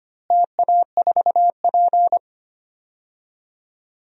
Morse code, heard as a keyed tone, about 25 wpm, 700 Hz. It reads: TA4P
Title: DXCC-100 Call Signs - 4 Characters - Repeated 3x - 25wpm